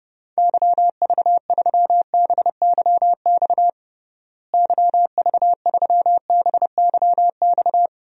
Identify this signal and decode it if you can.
Morse code, heard as a keyed tone, about 30 wpm, 700 Hz. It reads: YV3BYX YV3BYX
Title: DXCC-100 Call Signs - 6 Characters - Repeated 2x - 30wpm